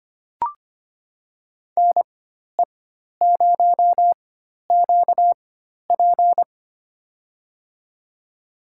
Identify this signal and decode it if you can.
Morse code, heard as a keyed tone, about 25 wpm, spaced as for 12 wpm, 700 Hz. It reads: NE0QP
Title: USA Call Signs - 5 Characters - 4x Character Spacing - 25wpm